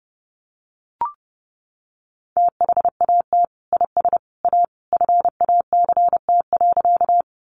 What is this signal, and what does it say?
Morse code, heard as a keyed tone, about 30 wpm, 700 Hz. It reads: THAT IS A FACT.
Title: Sentences from Top 500 Words 30wpm